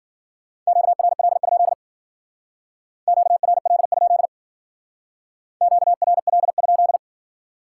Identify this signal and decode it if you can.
Morse code, heard as a keyed tone, about 48 wpm, 700 Hz. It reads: QRL? QRL? QRL?
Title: Speed-Racing Common QSO Phrases - 40wpm